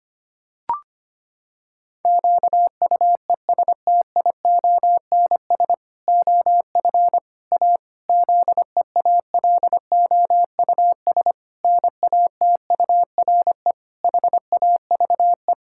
Morse code, heard as a keyed tone, about 25 wpm, 700 Hz. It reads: QUESTIONS OF A ZEALOUS NATURE HAVE
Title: Pangram Challenge v2 - 25wpm